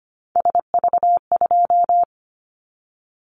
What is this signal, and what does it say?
Morse code, heard as a keyed tone, about 25 wpm, 700 Hz. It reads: SV2
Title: Callsign Prefixes for Top 50 DXCC - 25wpm